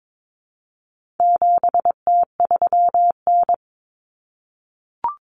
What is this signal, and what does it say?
Morse code, heard as a keyed tone, about 22 wpm, 700 Hz. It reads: 7T3N
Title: Common Cut Numbers - 4-Digit 22wpm